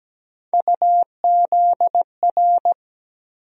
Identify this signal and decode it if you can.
Morse code, heard as a keyed tone, about 17 wpm, 700 Hz. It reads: UZR
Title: USA Call Sign Postfixes - 3 Letters 17wpm